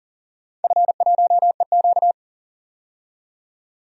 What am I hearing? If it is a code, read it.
Morse code, heard as a keyed tone, about 40 wpm, 700 Hz. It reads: F1EQ
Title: DXCC-100 Call Signs - 4 Characters - 40wpm